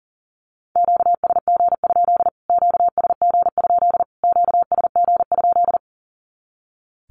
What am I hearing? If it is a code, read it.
Morse code, heard as a keyed tone, about 40 wpm, 700 Hz. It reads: QSG? QSG? QSG?
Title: Q-Codes Repeated 3x 40wpm